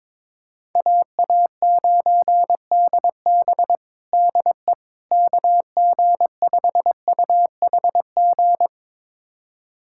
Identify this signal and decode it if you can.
Morse code, heard as a keyed tone, about 22 wpm, 700 Hz. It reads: AA9DB DE KG5UHG
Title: USA Callsign DE Callsign 22wpm